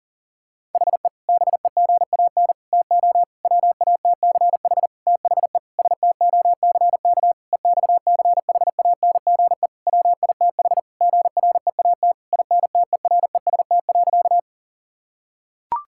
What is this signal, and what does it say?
Morse code, heard as a keyed tone, about 37 wpm, 700 Hz. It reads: HE BEGAN TO WATCH THE STOCK EXCHANGE WITH GREAT INTEREST.
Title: Sentences from Top 1000 Words - Review Part 6 - 40wpm